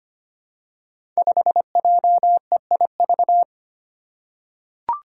Morse code, heard as a keyed tone, about 25 wpm, 700 Hz. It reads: 5JEIV
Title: Five Challenging Letter-Number Combinations 25wpm